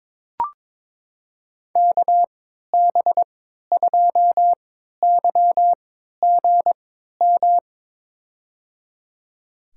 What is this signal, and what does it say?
Morse code, heard as a keyed tone, about 21 wpm, spaced as for 12 wpm, 700 Hz. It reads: KB2YGM
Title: USA Call Signs - 3x Character Spacing - 22wpm